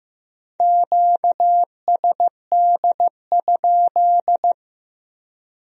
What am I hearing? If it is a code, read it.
Morse code, heard as a keyed tone, about 18 wpm, 700 Hz. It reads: QSD?
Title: Speed-Racing Q-Codes - 15wpm